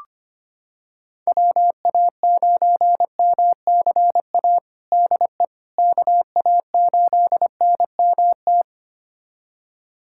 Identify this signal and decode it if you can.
Morse code, heard as a keyed tone, about 25 wpm, 700 Hz. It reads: WA9MCA DE KA8NMT
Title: USA Callsign DE Callsign - 6 Characters - 25wpm